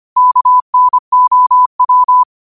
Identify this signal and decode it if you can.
Morse code, heard as a keyed tone, about 25 wpm, 1000 Hz. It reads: KNOW
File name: qotdfiles / 25wpm